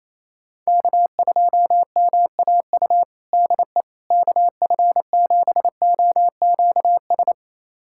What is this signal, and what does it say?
Morse code, heard as a keyed tone, about 28 wpm, 700 Hz. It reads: K2MAU DE KF7OQS
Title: USA Callsign DE Callsign 28wpm